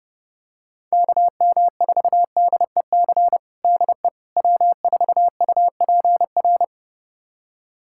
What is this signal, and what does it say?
Morse code, heard as a keyed tone, about 30 wpm, 700 Hz. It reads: KM4DEC DE W4UPR
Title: USA Callsign DE Callsign 30wpm